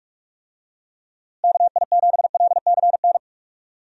Morse code, heard as a keyed tone, about 46 wpm, 700 Hz. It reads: KI7LCN
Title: USA Call Signs - 6 Characters - Repeated 2x - 45wpm